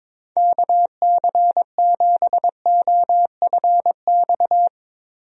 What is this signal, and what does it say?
Morse code, heard as a keyed tone, about 22 wpm, 700 Hz. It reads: KC7OFX
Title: USA Call Signs - 6 Characters - Repeated 3x - 22wpm